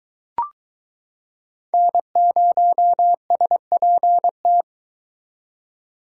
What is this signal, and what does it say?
Morse code, heard as a keyed tone, about 22 wpm, 700 Hz. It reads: N0SPT